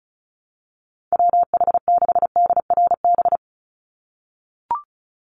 Morse code, heard as a keyed tone, about 34 wpm, 700 Hz. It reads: WH6DRB
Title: USA Call Signs 35wpm